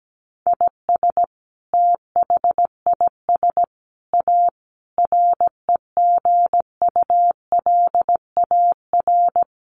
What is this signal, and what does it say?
Morse code, heard as a keyed tone, about 17 wpm, 700 Hz. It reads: IS THIS A REGULAR
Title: Sentences from Top 900 Words 17wpm